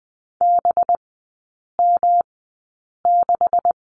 Morse code, heard as a keyed tone, about 20 wpm, spaced as for 8 wpm, 700 Hz. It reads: BM6
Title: 191 Lesson 31e - Letter v - Three Characters 20wpm